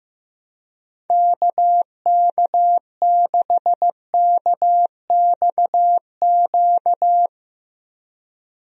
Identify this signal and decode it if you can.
Morse code, heard as a keyed tone, about 15 wpm, 700 Hz. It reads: KK6KXQ
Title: USA Call Signs - 6 Characters - Repeated 3x - 15wpm